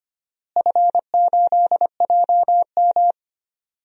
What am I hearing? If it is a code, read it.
Morse code, heard as a keyed tone, about 25 wpm, 700 Hz. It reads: F8JM
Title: DXCC-100 Call Signs - 4 Characters - Repeated 2x - 25wpm